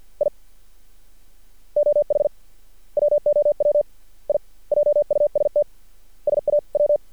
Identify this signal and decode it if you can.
Morse code, heard as a keyed tone, about 50 wpm, 590 Hz. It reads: I OH WOW I JUST SAW
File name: archived / CRSnet